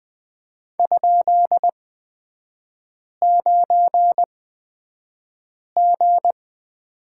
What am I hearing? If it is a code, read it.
Morse code, heard as a keyed tone, about 20 wpm, spaced as for 5 wpm, 700 Hz. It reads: ?9G
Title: 255 Lesson 41e - Period - Three Characters 20wpm